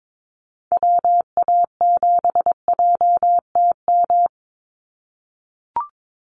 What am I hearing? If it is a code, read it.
Morse code, heard as a keyed tone, about 22 wpm, 700 Hz. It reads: WA7JTM
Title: NA SOTA Chaser Callsigns 22wpm